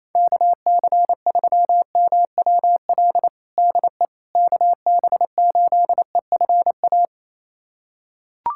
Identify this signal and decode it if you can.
Morse code, heard as a keyed tone, about 28 wpm, 700 Hz. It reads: KC3MWL DE KB8EFA
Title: USA Callsign DE Callsign - 6 Characters - 28wpm